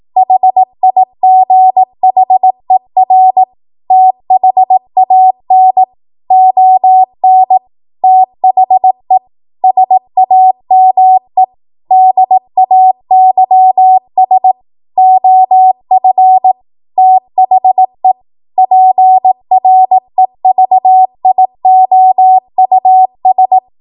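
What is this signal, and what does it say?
Morse code, heard as a keyed tone, about 18 wpm, 750 Hz. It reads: HIGHER THAN ON THE SAME DAYS OF THE PREVIOUS
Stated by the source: ARRL 18 wpm Morse code practice archive transmission.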